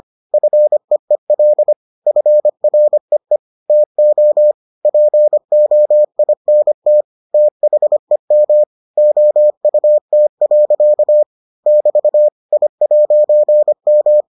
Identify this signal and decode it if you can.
Morse code, heard as a keyed tone, about 25 wpm, 600 Hz. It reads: FEEL FREE TO POINT THEM OUT. <BT> I'M